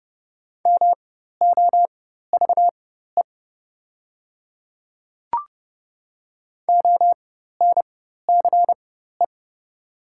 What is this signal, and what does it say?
Morse code, heard as a keyed tone, about 31 wpm, spaced as for 14 wpm, 700 Hz. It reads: MOVE ONCE
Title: 4x Character Spacing - Top 500 Words - 30wpm